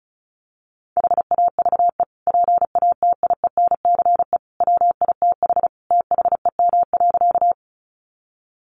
Morse code, heard as a keyed tone, about 34 wpm, 700 Hz. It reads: HAVE PATIENCE WITH THEM.
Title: Sentences from Top 1300 Words 35wpm